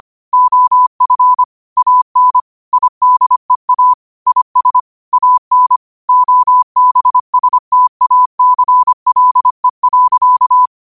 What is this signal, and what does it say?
Morse code, heard as a keyed tone, about 25 wpm, 1000 Hz. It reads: OF AN IDEA IS AN OBSTACLE.
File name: qotdfiles / 25wpm